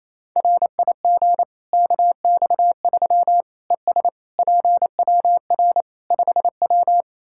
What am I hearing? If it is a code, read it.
Morse code, heard as a keyed tone, about 28 wpm, 700 Hz. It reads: RIG KX3 ES PWR 5W
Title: Practice QSO - Exchange 2 - Info - 28wpm